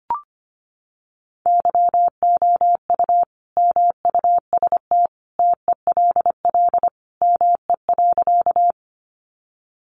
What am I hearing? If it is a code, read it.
Morse code, heard as a keyed tone, about 25 wpm, 700 Hz. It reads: YOU MUST TELL ME.
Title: Sentences from Top 200 Words 25wpm